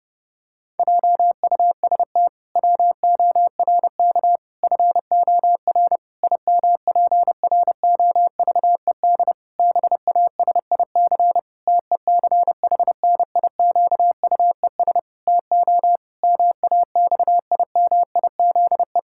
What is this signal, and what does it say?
Morse code, heard as a keyed tone, about 30 wpm, 700 Hz. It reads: JUST WORK FOR IMPROVED BASIC TECHNIQUES TO MAXIMIZE
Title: Pangram Challenge v2 - 30wpm